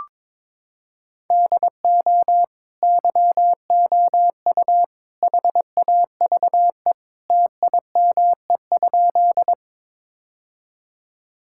Morse code, heard as a keyed tone, about 22 wpm, 700 Hz. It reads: DO YOU HAVE TIME?
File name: practice / 22wpm / Sentences with 3-Word Phrases - 22wpm